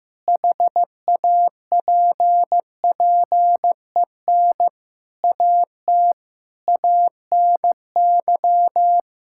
Morse code, heard as a keyed tone, about 15 wpm, 700 Hz. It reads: HAPPEN AT ANY
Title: Sentences from Top 1000 Words - Review Part 6 - 15wpm